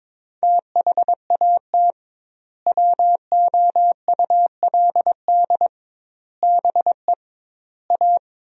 Morse code, heard as a keyed tone, about 21 wpm, 700 Hz. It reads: THAT WOULD BE A
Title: 2x Word-Spacing - Sentences from Top 500 Words - 22wpm